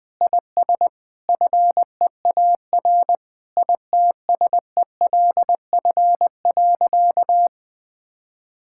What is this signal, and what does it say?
Morse code, heard as a keyed tone, about 21 wpm, 700 Hz. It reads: IS FEAR ITSELF.
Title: Sentences with 8-Word Phrases - Spoken Beginning - 20wpm